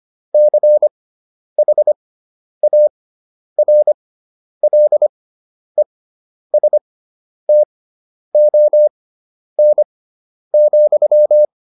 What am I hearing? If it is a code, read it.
Morse code, heard as a keyed tone, about 25 wpm, spaced as for 10 wpm, 600 Hz. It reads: CHARLESTON,
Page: intermediate-practice-files